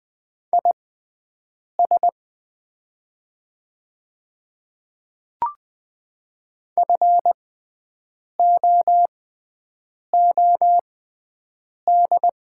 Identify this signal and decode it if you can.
Morse code, heard as a keyed tone, about 20 wpm, spaced as for 7 wpm, 700 Hz. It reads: IS FOOD
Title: 6x Character Spacing - Top 500 Words - 20wpm